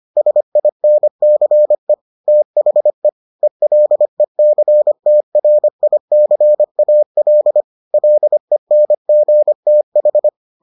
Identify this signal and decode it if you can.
Morse code, heard as a keyed tone, about 25 wpm, 600 Hz. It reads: SINCE THE ELECTRICAL LENGTH